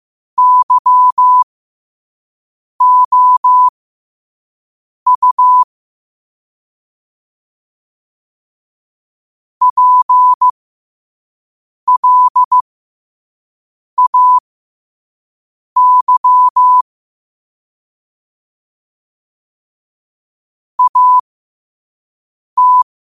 Morse code, heard as a keyed tone, about 15 wpm, spaced as for 5 wpm, 1000 Hz. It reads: YOU PLAY AT
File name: qotdfiles / 5wpm